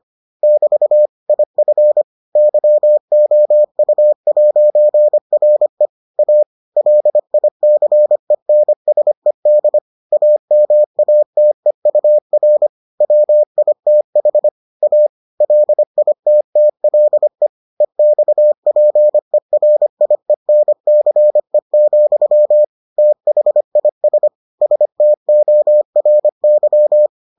Morse code, heard as a keyed tone, about 25 wpm, 600 Hz. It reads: <BT> IF YOU'RE A LICENSED AMATEUR WITH A LITTLE EXPERIENCE, THIS STORY